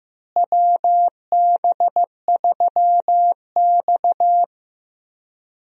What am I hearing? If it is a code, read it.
Morse code, heard as a keyed tone, about 15 wpm, 700 Hz. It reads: WB3X
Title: USA Call Signs - 4 Characters - Repeated 3x - 15wpm